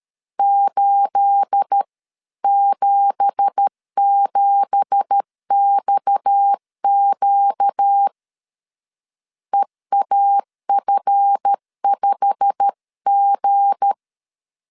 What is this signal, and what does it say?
Morse code, heard as a keyed tone, about 13 wpm, 800 Hz. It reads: 877XQ EAF5G
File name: CW / DET / 1 9 1500C S B 12wpm 3819